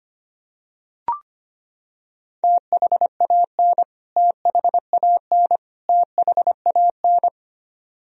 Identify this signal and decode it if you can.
Morse code, heard as a keyed tone, about 25 wpm, 700 Hz. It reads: THAN THAN THAN
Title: Top 100 Words Repeated 25wpm